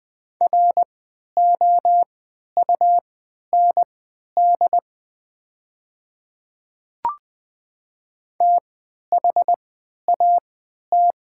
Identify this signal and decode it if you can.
Morse code, heard as a keyed tone, about 20 wpm, spaced as for 11 wpm, 700 Hz. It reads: ROUND THAT
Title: 3x Character Spacing - Top 200 Words - 20wpm